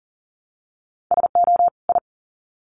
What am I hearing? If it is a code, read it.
Morse code, heard as a keyed tone, about 40 wpm, 700 Hz. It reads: SO I
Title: Sets of 2 Words 40wpm